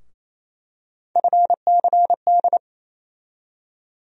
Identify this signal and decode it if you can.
Morse code, heard as a keyed tone, about 28 wpm, 700 Hz. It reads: FCD
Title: USA Call Sign Postfixes - 3 Letters 28wpm